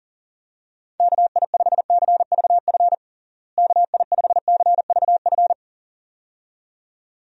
Morse code, heard as a keyed tone, about 40 wpm, 700 Hz. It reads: KI5CVF KI5CVF
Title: USA Call Signs - 6 Characters - Repeated 2x - 40wpm